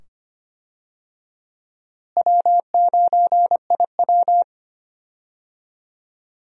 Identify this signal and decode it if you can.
Morse code, heard as a keyed tone, about 25 wpm, 700 Hz. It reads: W9IW
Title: USA Call Signs - 4 Characters - Repeated 2x - 25wpm